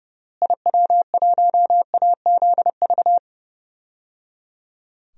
Morse code, heard as a keyed tone, about 30 wpm, 700 Hz. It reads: IW1AZV
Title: DXCC-100 Call Signs - 6 Characters - Repeated 3x - 30wpm